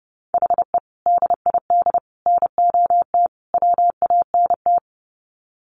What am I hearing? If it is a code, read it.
Morse code, heard as a keyed tone, about 30 wpm, 700 Hz. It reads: HE DID NOT WANT
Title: Sets of 4 Words 30wpm